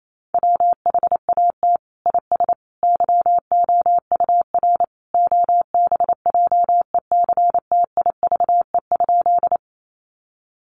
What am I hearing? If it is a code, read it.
Morse code, heard as a keyed tone, about 28 wpm, 700 Hz. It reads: WHAT IS YOUR OBJECTIVE?
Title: Sentences from Top 1200 Words 28wpm